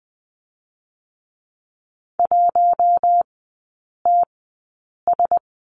Morse code, heard as a keyed tone, about 20 wpm, spaced as for 8 wpm, 700 Hz. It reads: 1TS
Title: 053 Lesson 10e - Letter r - Three Characters 20wpm